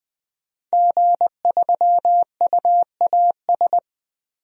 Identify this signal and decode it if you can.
Morse code, heard as a keyed tone, about 20 wpm, 700 Hz. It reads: G3UAS
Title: DXCC-100 Call Signs - 5 Characters - 20wpm